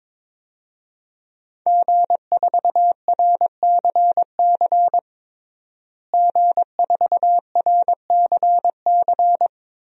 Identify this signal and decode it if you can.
Morse code, heard as a keyed tone, about 22 wpm, 700 Hz. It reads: G4RCC G4RCC
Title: DXCC-100 Call Signs - 5 Characters - Repeated 2x - 22wpm